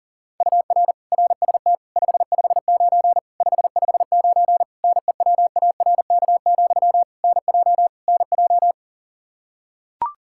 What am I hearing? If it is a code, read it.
Morse code, heard as a keyed tone, about 40 wpm, 700 Hz. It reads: UR RST 559 559 NEWARK, NJ NJ
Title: Practice QSO - Exchange 1 - Beginning - 40wpm